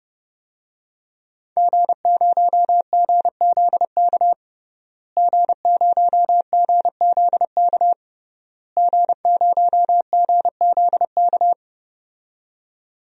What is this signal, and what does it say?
Morse code, heard as a keyed tone, about 31 wpm, 700 Hz. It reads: G0GZK G0GZK G0GZK
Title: DXCC-100 Call Signs - 5 Characters - Repeated 3x - 30wpm